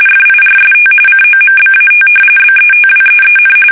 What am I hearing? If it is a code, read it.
RTTY, about 45.45 baud. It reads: FIRST COMBAT BETWEEN
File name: fsk850 / usnavy3